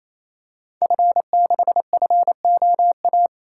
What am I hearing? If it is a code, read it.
Morse code, heard as a keyed tone, about 27 wpm, 700 Hz. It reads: F6FOA
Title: Code Groups - 5 Letter:Numbers - 28wpm